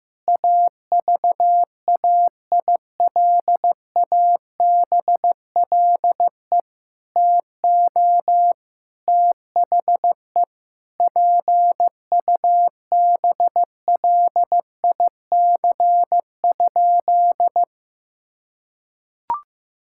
Morse code, heard as a keyed tone, about 15 wpm, 700 Hz. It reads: AVAILABLE TO THE PUBLIC?
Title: Sentences from Top 1000 Words - Review Part 5 - 15wpm